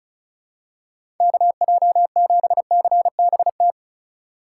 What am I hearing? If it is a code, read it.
Morse code, heard as a keyed tone, about 36 wpm, 700 Hz. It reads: KJ7CBT
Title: USA Call Signs - 6 Characters - 3x Character Spacing - 35wpm